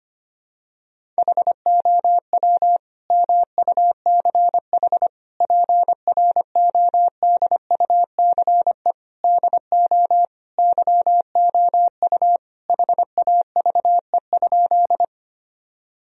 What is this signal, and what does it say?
Morse code, heard as a keyed tone, about 25 wpm, 700 Hz. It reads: HOW MUCH PRODUCE DO YOU HAVE?
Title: Sentences from Top 1000 Words - Review Part 3 - 25wpm